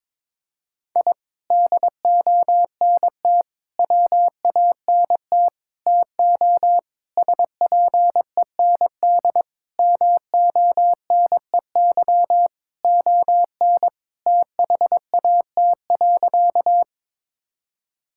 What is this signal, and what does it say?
Morse code, heard as a keyed tone, about 22 wpm, 700 Hz. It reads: I DONT WANT TO SPEND MONEY ON THAT.
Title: Sentences from Top 1000 Words - Review Part 5 - 22wpm